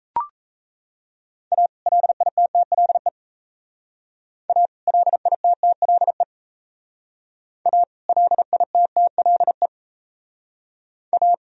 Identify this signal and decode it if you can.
Morse code, heard as a keyed tone, about 35 wpm, 700 Hz. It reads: A LITTLE A LITTLE A LITTLE A
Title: Speed-Racing - Sets of 2 Words 28wpm